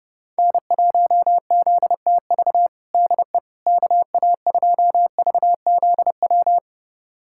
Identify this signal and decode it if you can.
Morse code, heard as a keyed tone, about 30 wpm, 700 Hz. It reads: N1ZTV DE KA2VZW
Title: USA Callsign DE Callsign 30wpm